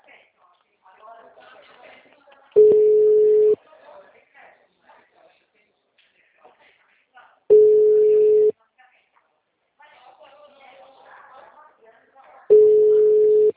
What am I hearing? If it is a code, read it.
Ringback tone (Europe, 425 Hz), with speech.
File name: out- -2010- - - .28894